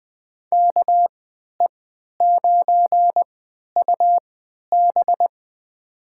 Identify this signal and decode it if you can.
Morse code, heard as a keyed tone, about 20 wpm, spaced as for 11 wpm, 700 Hz. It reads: KE9UB
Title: USA Call Signs - 5 Characters - 3x Character Spacing - 20wpm